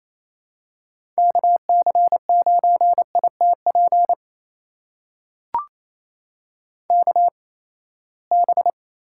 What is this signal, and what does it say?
Morse code, heard as a keyed tone, about 28 wpm, spaced as for 19 wpm, 700 Hz. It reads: KC9ITP KB
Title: USA Call Signs - 6 Characters - 8x Character Spacing - 28wpm